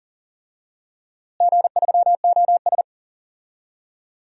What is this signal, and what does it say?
Morse code, heard as a keyed tone, about 40 wpm, 700 Hz. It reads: G3OS
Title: DXCC-100 Call Signs - 4 Characters - Repeated 2x - 40wpm